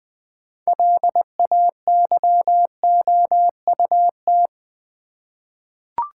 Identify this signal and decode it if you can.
Morse code, heard as a keyed tone, about 20 wpm, 700 Hz. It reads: LAYOUT
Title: 6 Letter Words - First 3 Letters Given - 20wpm